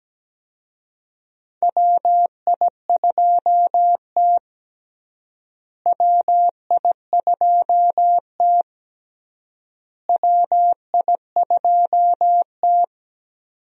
Morse code, heard as a keyed tone, about 17 wpm, 700 Hz. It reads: WI2T WI2T WI2T
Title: USA Call Signs - 4 Characters - Repeated 3x - 17wpm